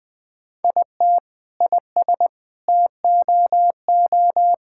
Morse code, heard as a keyed tone, about 20 wpm, 700 Hz. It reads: IT IS TOO
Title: Sentences from Top 1000 Words - Review Part 6 - 20wpm